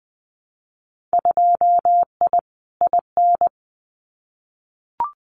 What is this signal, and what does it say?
Morse code, heard as a keyed tone, about 20 wpm, 700 Hz. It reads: 2I IN
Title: Winter Field Day Exchanges Encore 20wpm